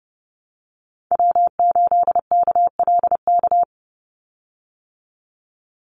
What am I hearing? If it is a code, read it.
Morse code, heard as a keyed tone, about 30 wpm, 700 Hz. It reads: W8KLK
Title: USA Call Signs 30wpm